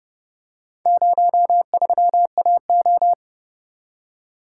Morse code, heard as a keyed tone, about 29 wpm, 700 Hz. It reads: 03AO